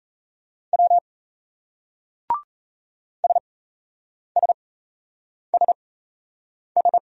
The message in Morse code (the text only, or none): W SSSS